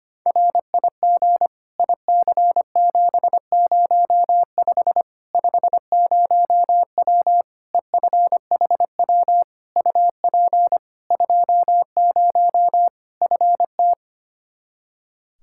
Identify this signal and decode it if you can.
Morse code, heard as a keyed tone, about 25 wpm, 700 Hz. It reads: RIG IC705 50W EFHW UP 20 FT
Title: Practice QSO - Exchange 2 - Info - 25wpm